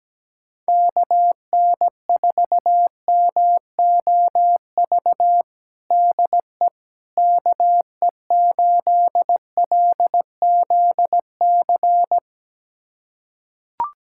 Morse code, heard as a keyed tone, about 17 wpm, 700 Hz. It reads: KN4MOV DE KE8LZC
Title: USA Callsign DE Callsign - 6 Characters - 17wpm